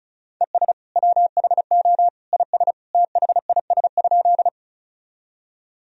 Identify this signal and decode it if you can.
Morse code, heard as a keyed tone, about 36 wpm, 700 Hz. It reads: ES WHO IS THIS?
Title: Sentences from Top 100 Words and Abbreviations 35wpm